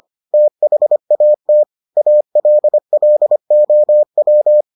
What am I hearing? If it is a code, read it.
Morse code, heard as a keyed tone, about 25 wpm, 600 Hz. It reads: THAT ALLOW